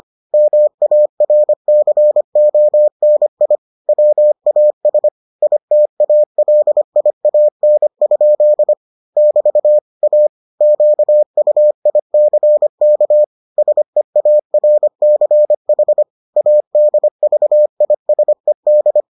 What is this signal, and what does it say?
Morse code, heard as a keyed tone, about 25 wpm, 600 Hz. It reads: MARCONI WAS ITALIAN? <BT> A QUICK SEARCH ADVISED